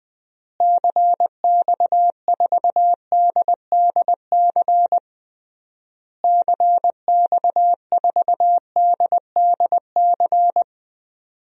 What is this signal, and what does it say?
Morse code, heard as a keyed tone, about 21 wpm, 700 Hz. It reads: CX4DDC CX4DDC
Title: DXCC-100 Call Signs - 6 Characters - Repeated 3x - 20wpm